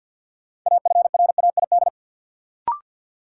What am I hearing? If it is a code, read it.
Morse code, heard as a keyed tone, about 52 wpm, 700 Hz. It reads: AFRAID
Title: Sentences from Top 1500 Words - New Word Drill - 50wpm